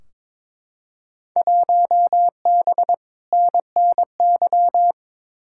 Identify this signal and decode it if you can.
Morse code, heard as a keyed tone, about 22 wpm, 700 Hz. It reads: 1B NNY
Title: Field Day Exchanges - 22wpm